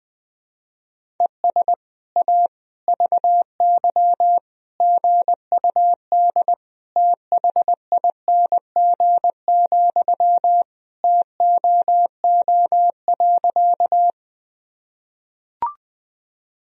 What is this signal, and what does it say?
Morse code, heard as a keyed tone, about 21 wpm, 700 Hz. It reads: ES A VY GUD THING, TOO.
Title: Sentences from Top 200 Words and Abbreviations 20wpm